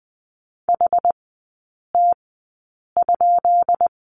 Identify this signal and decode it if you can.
Morse code, heard as a keyed tone, about 20 wpm, spaced as for 8 wpm, 700 Hz. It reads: HT?
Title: 133 Lesson 22e - Question Mark - Three Characters 20wpm